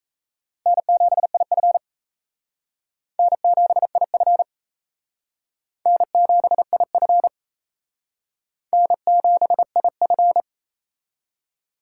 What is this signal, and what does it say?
Morse code, heard as a keyed tone, about 36 wpm, 700 Hz. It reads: N7IF N7IF N7IF N7IF